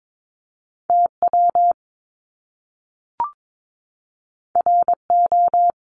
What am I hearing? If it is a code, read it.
Morse code, heard as a keyed tone, about 22 wpm, 700 Hz. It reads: TW RO
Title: Top Two Letter Combinations in Words 22wpm